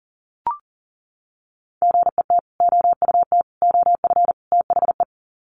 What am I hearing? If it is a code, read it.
Morse code, heard as a keyed tone, about 40 wpm, 700 Hz. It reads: GET OUT OF THE
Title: Sets of 4 Words 40wpm